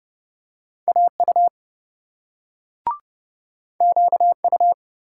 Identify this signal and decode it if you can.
Morse code, heard as a keyed tone, about 27 wpm, 700 Hz. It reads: AU QU